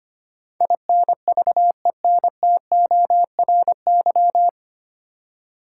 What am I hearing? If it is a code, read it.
Morse code, heard as a keyed tone, about 25 wpm, 700 Hz. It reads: INVENTORY
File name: Nine Letter Words - 25wpm